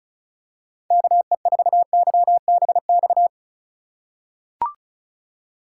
Morse code, heard as a keyed tone, about 36 wpm, 700 Hz. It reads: KE4YBX
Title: USA Call Signs - 6 Characters - 4x Character Spacing - 35wpm